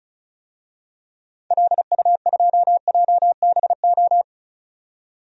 Morse code, heard as a keyed tone, about 36 wpm, 690 Hz. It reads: LU2JBO
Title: DXCC-100 Call Signs - 6 Characters - 35wpm